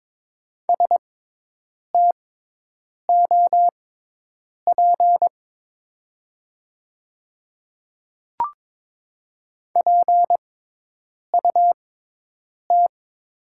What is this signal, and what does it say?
Morse code, heard as a keyed tone, about 22 wpm, spaced as for 8 wpm, 700 Hz. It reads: STOP PUT
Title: 6x Character Spacing - Top 500 Words - 22wpm